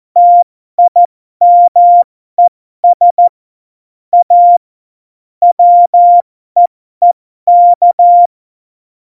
Morse code, heard as a keyed tone, about 14 wpm, spaced as for 12 wpm, 700 Hz. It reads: TIMES A WEEK
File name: Morsekurs / Witze